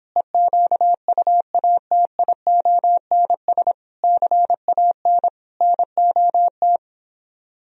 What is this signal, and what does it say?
Morse code, heard as a keyed tone, about 26 wpm, 700 Hz. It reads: EQUATIONS CAN NOT